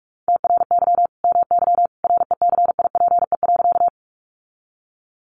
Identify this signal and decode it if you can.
Morse code, heard as a keyed tone, about 44 wpm, 700 Hz. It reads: TRY MY RECIPE.
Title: Sentences from Top 1200 Words 45wpm